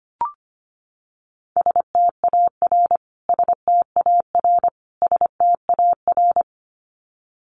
Morse code, heard as a keyed tone, about 25 wpm, 700 Hz. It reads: STAR STAR STAR
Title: Top 500 Words Repeated 25wpm